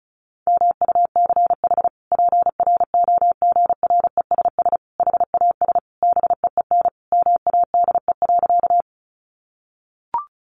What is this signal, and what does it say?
Morse code, heard as a keyed tone, about 34 wpm, 700 Hz. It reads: MUCH PROGRESS HAS BEEN MADE.
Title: Sentences from Top 1400 Words 35wpm